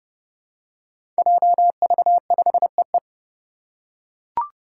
Morse code, heard as a keyed tone, about 30 wpm, 700 Hz. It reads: JV5EE
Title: Five Challenging Letter-Number Combinations 30wpm